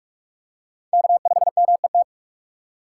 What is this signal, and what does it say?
Morse code, heard as a keyed tone, about 44 wpm, 700 Hz. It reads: K5MET